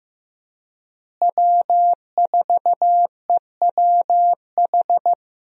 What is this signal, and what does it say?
Morse code, heard as a keyed tone, about 15 wpm, 700 Hz. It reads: W4EWH
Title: USA Call Signs - 5 Characters - Repeated 3x - 15wpm